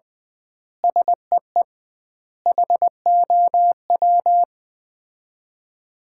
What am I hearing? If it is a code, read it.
Morse code, heard as a keyed tone, about 21 wpm, 700 Hz. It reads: SEE HOW
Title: 2x Word-Spacing - Sets of 5 Words - 20wpm